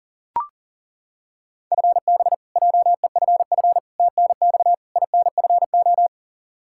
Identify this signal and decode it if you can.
Morse code, heard as a keyed tone, about 40 wpm, 700 Hz. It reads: FB JEFF TNX INFO
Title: Practice QSO - Exchange 3 - More Info - 40wpm